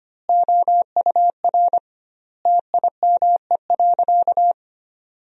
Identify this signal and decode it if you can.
Morse code, heard as a keyed tone, about 25 wpm, 700 Hz. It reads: OUR TIME.
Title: 2x Word-Spacing - Sentences from Top 200 Words - 25wpm